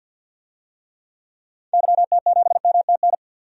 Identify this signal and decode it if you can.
Morse code, heard as a keyed tone, about 46 wpm, 700 Hz. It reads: YT7MTN
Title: DXCC-100 Call Signs - 6 Characters - 50wpm